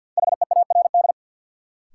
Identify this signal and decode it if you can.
Morse code, heard as a keyed tone, about 50 wpm, 700 Hz. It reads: HEARD
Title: Sentences from Top 400 Words - New Word Drill - 50wpm